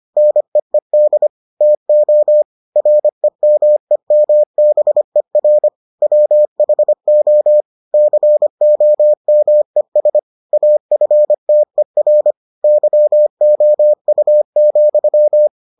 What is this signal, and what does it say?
Morse code, heard as a keyed tone, about 25 wpm, 600 Hz. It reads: NEED TO REMEMBER WHO COMES AFTER YOU,